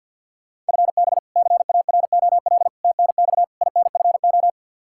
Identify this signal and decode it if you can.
Morse code, heard as a keyed tone, about 48 wpm, 700 Hz. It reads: FB CAROL TNX INFO
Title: K1USN SST QSO Practice 50wpm